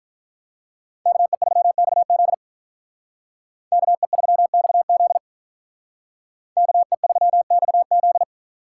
Morse code, heard as a keyed tone, about 45 wpm, 700 Hz. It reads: KE3XZ KE3XZ KE3XZ